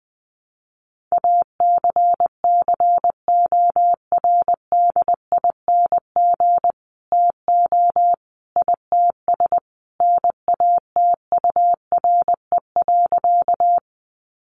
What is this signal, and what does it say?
Morse code, heard as a keyed tone, about 20 wpm, 700 Hz. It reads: ACCORDING TO ITS NATURE.
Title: Sentences from Top 700 Words 20wpm